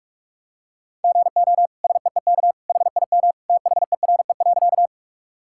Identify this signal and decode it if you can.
Morse code, heard as a keyed tone, about 47 wpm, 700 Hz. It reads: GO SEEK HIM THERE.
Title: Sentences from Top 1600 Words 45wpm